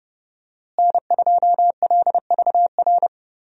Morse code, heard as a keyed tone, about 32 wpm, 700 Hz. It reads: N2LVR
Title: USA Call Signs - 5 Characters - Repeated 3x - 30wpm